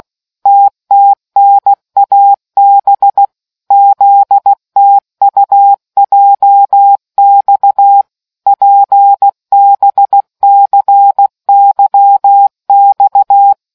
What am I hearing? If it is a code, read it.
Morse code, heard as a keyed tone, about 16 wpm, 800 Hz. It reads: TTNAB ZTUJX PBCYX